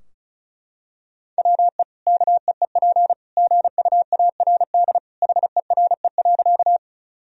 Morse code, heard as a keyed tone, about 34 wpm, 700 Hz. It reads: WE KEEP GUARD HERE.
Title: Sentences from Top 1000 Words - Review Part 4 - 35wpm